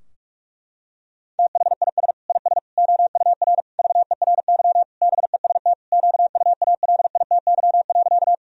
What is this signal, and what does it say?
Morse code, heard as a keyed tone, about 47 wpm, 700 Hz. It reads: THIS IS OUR VERY BEST QUALITY.
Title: Sentences from Top 1000 Words - Review Part 4 - 45wpm